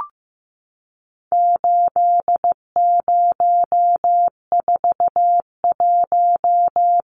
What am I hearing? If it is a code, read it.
Morse code, heard as a keyed tone, about 15 wpm, 700 Hz. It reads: 8041
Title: Numbers - 4-Digit 15wpm